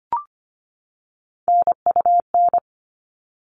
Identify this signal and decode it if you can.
Morse code, heard as a keyed tone, about 25 wpm, 700 Hz. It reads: NUN
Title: All Cut Numbers - 3-Digit 25wpm